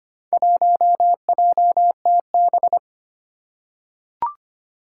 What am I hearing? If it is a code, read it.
Morse code, heard as a keyed tone, about 25 wpm, 700 Hz. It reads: 1JTB
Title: Four Challenging Letter-Number Combinations 25wpm